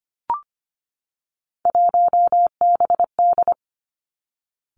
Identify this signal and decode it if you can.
Morse code, heard as a keyed tone, about 25 wpm, 700 Hz. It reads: 1BD
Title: Three Challenging Letter-Number Combinations 25wpm